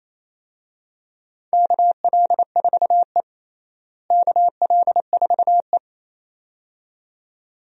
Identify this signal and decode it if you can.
Morse code, heard as a keyed tone, about 28 wpm, 700 Hz. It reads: KL4E KL4E
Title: USA Call Signs - 4 Characters - Repeated 2x - 28wpm